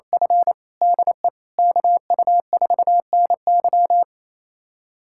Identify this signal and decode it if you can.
Morse code, heard as a keyed tone, about 28 wpm, 700 Hz. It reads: F DE KU4NY
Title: USA Callsign DE Callsign - 5 Characters - 28wpm